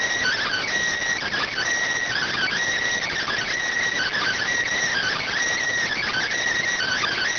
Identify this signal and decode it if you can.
Emergency Alert System (SAME) tones (header bursts), heard as harsh buzzing bursts. It is heard from 0.6 to 6.4 s.